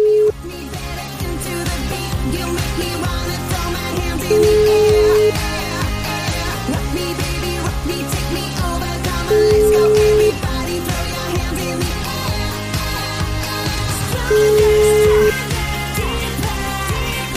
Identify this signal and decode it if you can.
Ringback tone (Europe, 425 Hz), with noise.